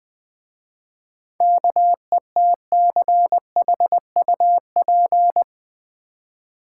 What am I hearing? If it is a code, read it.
Morse code, heard as a keyed tone, about 20 wpm, 700 Hz. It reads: KETCHUP
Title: Easy Words with Hints - 20wpm